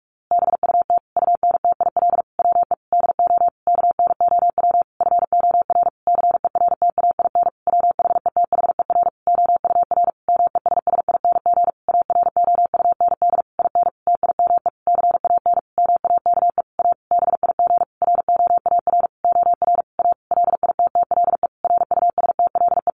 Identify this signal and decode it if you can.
Morse code, heard as a keyed tone, about 44 wpm, 700 Hz. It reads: BUT UNTIL WE DO KNOW FOR CERTAIN WHETHER OUR MESSING AROUND IN TIME CAN MAKE A BIG ROAR OR A LITTLE RUSTLE
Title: A Sound Of Thunder 45wpm